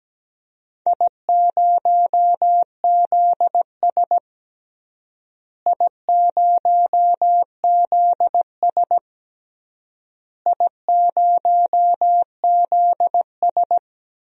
Morse code, heard as a keyed tone, about 17 wpm, 700 Hz. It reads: I0ZS I0ZS I0ZS
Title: DXCC-100 Call Signs - 4 Characters - Repeated 3x - 17wpm